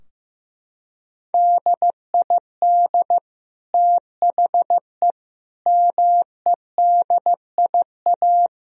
Morse code, heard as a keyed tone, about 15 wpm, 700 Hz. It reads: DID THE MEDIA
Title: Sentences from Top 1000 Words - Review Part 4 - 15wpm